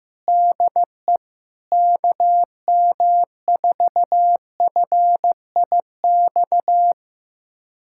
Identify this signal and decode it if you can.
Morse code, heard as a keyed tone, about 15 wpm, 700 Hz. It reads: DE KM4FIX
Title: USA Callsign DE Callsign 15wpm